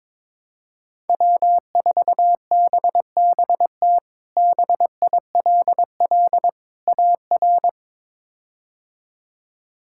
Morse code, heard as a keyed tone, about 22 wpm, 700 Hz. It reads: W4BBT BILL AR
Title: K1USN SST QSO Practice - 22wpm